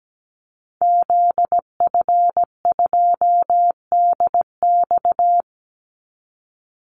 Morse code, heard as a keyed tone, about 17 wpm, 700 Hz. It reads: ZF2DX
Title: Call Signs - 17wpm